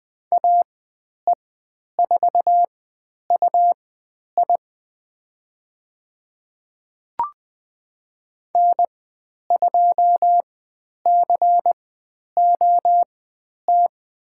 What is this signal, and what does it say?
Morse code, heard as a keyed tone, about 20 wpm, spaced as for 10 wpm, 700 Hz. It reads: AE4UI N2COT
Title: 109 Lesson 18g - Letter m - Call Signs 20wpm